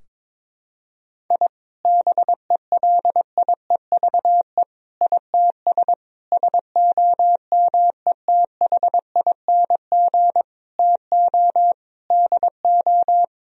Morse code, heard as a keyed tone, about 22 wpm, 700 Hz. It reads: I BELIEVE ITS SOMETHING TO DO
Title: Sentences from Top 1000 Words - Review Part 3 - 22wpm